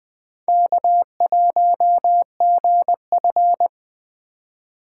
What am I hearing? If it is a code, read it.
Morse code, heard as a keyed tone, about 20 wpm, 700 Hz. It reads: K1GF
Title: Speed Racing - USA 1x2 Call Signs 17wpm